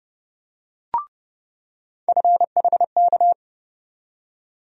Morse code, heard as a keyed tone, about 30 wpm, 700 Hz. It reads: FHK
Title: USA Call Sign Postfixes - 3 Letters 30wpm